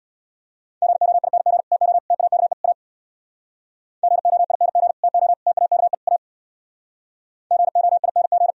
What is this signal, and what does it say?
Morse code, heard as a keyed tone, about 76 wpm, 700 Hz. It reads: GOING TO TAKE A GOING TO TAKE A GOING